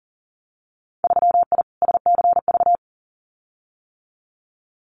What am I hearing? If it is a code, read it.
Morse code, heard as a keyed tone, about 40 wpm, 700 Hz. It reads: 3I SCV
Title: Winter Field Day Exchanges Encore 40wpm